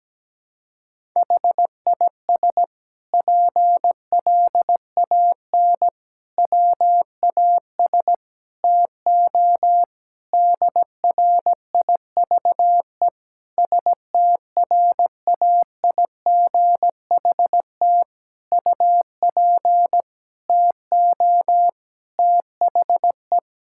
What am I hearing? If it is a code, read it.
Morse code, heard as a keyed tone, about 17 wpm, 700 Hz. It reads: HIS PLAN WAS TO DRIVE STRAIGHT UP TO THE
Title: Sentences from Top 1000 Words - Review Part 6 - 17wpm